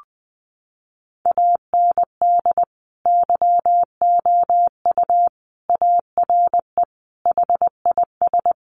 Morse code, heard as a keyed tone, about 20 wpm, 700 Hz. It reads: AND YOU ARE HIS
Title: Sentences from Top 1300 Words 20wpm